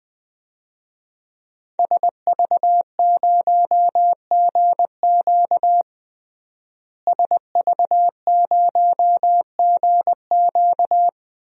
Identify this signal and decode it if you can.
Morse code, heard as a keyed tone, about 21 wpm, 700 Hz. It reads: SV0GQ SV0GQ
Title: DXCC-100 Call Signs - 5 Characters - Repeated 3x - 20wpm